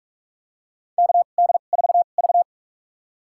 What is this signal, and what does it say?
Morse code, heard as a keyed tone, about 42 wpm, spaced as for 31 wpm, 700 Hz. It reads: KD4V
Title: USA Call Signs - 4 Characters - 2x Character Spacing - 45wpm